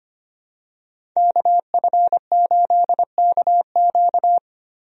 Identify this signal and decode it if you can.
Morse code, heard as a keyed tone, about 25 wpm, 700 Hz. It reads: KF8KQ
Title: USA Call Signs - 5 Characters - 25wpm